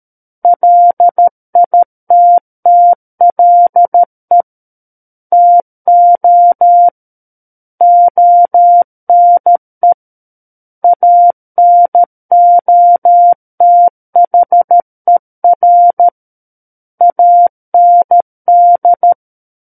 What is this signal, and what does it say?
Morse code, heard as a keyed tone, about 13 wpm, 700 Hz. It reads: LITTLE TO ONE ANOTHER AND
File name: Book One / 04-Chapter 4 - 13 WPM